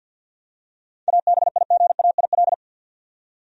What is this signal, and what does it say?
Morse code, heard as a keyed tone, about 48 wpm, 700 Hz. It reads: ABIGAIL
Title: USA Common Names - 50wpm